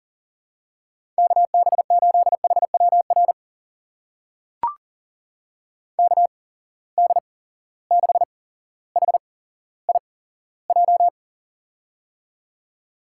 Morse code, heard as a keyed tone, about 38 wpm, spaced as for 17 wpm, 700 Hz. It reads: KB8HWR KD6HIJ
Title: USA Call Signs - 6 Characters - 8x Character Spacing - 40wpm